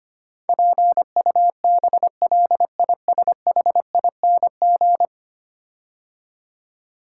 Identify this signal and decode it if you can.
Morse code, heard as a keyed tone, about 25 wpm, 690 Hz. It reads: PUBLISHING
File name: 10 Letter Words - First 3 Letters Given - 25wpm